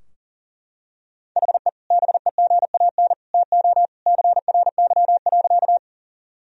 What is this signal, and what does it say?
Morse code, heard as a keyed tone, about 40 wpm, 700 Hz. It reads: HE BEGAN TO CRY.
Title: Sentences from Top 1000 Words - Review Part 2 - 40wpm